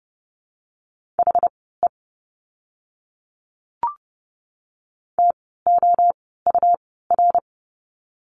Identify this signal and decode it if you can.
Morse code, heard as a keyed tone, about 30 wpm, spaced as for 17 wpm, 700 Hz. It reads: HE TOUR
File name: Lesson 17f - Letter u - Words 30wpm